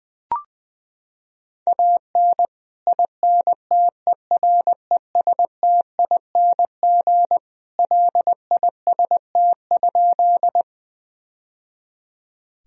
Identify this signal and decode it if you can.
Morse code, heard as a keyed tone, about 21 wpm, 700 Hz. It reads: AN INTERESTING LIST?
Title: Sentences from Top 1000 Words - Review Part 2 - 20wpm